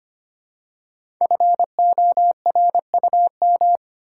Morse code, heard as a keyed tone, about 25 wpm, 700 Hz. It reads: FORUM
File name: 5 Letter Words - First 3 Letters Given - 25wpm